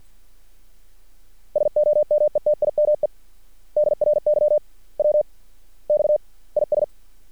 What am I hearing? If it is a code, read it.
Morse code, heard as a keyed tone, about 50 wpm, spaced as for 37 wpm, 600 Hz. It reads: SOMETIME DRY WX IS
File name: archived / CRSnet